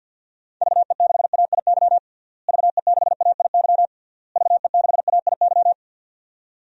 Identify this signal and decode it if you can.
Morse code, heard as a keyed tone, about 53 wpm, 700 Hz. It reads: VE6AIY VE6AIY VE6AIY
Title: DXCC-100 Call Signs - 6 Characters - Repeated 3x - 50wpm